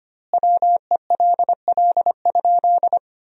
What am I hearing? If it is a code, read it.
Morse code, heard as a keyed tone, about 25 wpm, 700 Hz. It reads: WELL?
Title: 2x Word-Spacing - Sentences from Top 400 Words - 25wpm